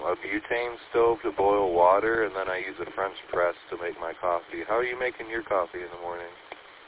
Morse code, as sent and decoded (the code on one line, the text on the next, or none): none